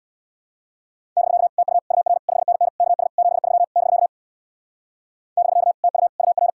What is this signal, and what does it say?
Morse code, heard as a keyed tone, about 64 wpm, 700 Hz. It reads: <BK> TU UR 5NN KS 73 <BK> <BK> TU UR